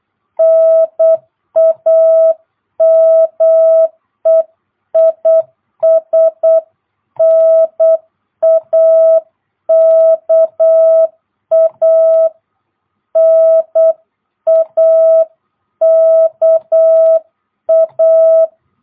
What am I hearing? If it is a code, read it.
Morse code, heard as a keyed tone, about 8 wpm, 660 Hz. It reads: NAME IS NAKA NAKA
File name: jh8cbh / 2022cw